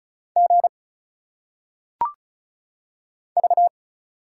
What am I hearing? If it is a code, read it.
Morse code, heard as a keyed tone, about 36 wpm, 700 Hz. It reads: G V
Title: Single Letters - Phonetic Pronunciation 35wpm